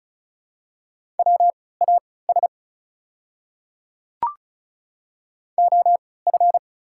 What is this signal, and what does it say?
Morse code, heard as a keyed tone, about 36 wpm, spaced as for 20 wpm, 700 Hz. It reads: WAS OF
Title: 3x Character Spacing - Top 200 Words - 35wpm